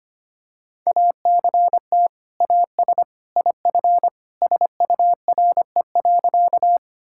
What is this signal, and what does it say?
Morse code, heard as a keyed tone, about 25 wpm, 700 Hz. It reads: ACT AS IF SURE.
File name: practice / 25wpm / Sentences with 2-Word Phrases - 25wpm